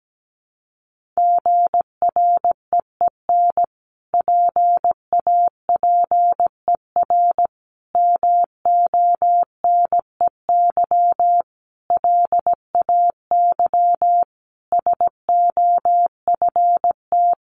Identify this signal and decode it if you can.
Morse code, heard as a keyed tone, about 17 wpm, 700 Hz. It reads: GREEN PAPER MONEY LAY SOFT
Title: Sentences from Top 1500 Words 17wpm